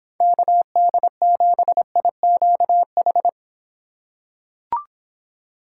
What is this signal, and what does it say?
Morse code, heard as a keyed tone, about 25 wpm, 700 Hz. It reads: KD7IQH